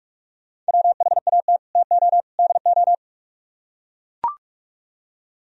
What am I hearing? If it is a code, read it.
Morse code, heard as a keyed tone, about 44 wpm, 700 Hz. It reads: WHAT TO DO